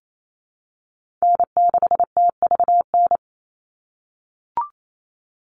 Morse code, heard as a keyed tone, about 28 wpm, 700 Hz. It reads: N6TVN
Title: Call Signs - 28wpm